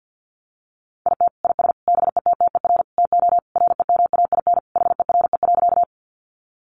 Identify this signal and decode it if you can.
Morse code, heard as a keyed tone, about 50 wpm, 700 Hz. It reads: IT IS BETTER TO REMAIN HERE.
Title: Sentences from Top 1200 Words 50wpm